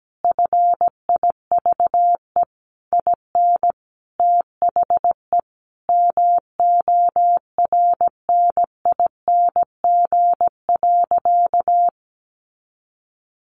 Morse code, heard as a keyed tone, about 17 wpm, 700 Hz. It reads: FIVE IN THE MORNING.
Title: Sentences from Top 1300 Words 17wpm